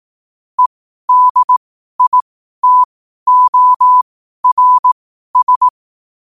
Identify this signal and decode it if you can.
Morse code, heard as a keyed tone, about 18 wpm, spaced as for 12 wpm, 1000 Hz. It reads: EDITORS
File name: qotdfiles / 12wpm